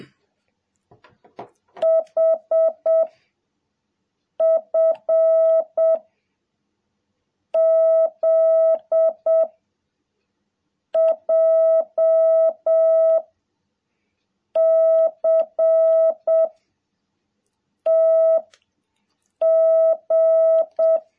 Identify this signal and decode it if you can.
Morse code, heard as a keyed tone, about 7 wpm, 650 Hz. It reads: H F Z J C T G
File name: jh8cbh / 2021cw